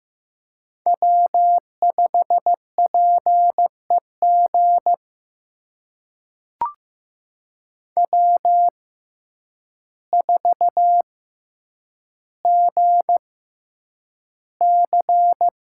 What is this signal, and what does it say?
Morse code, heard as a keyed tone, about 16 wpm, spaced as for 8 wpm, 700 Hz. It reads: W5PEG W4GC
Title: USA Call Signs - 5 Characters - 6x Character Spacing - 15wpm